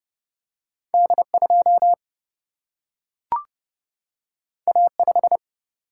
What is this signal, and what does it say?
Morse code, heard as a keyed tone, about 31 wpm, 700 Hz. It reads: D2 A5
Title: Single-Letter Single-Number 30wpm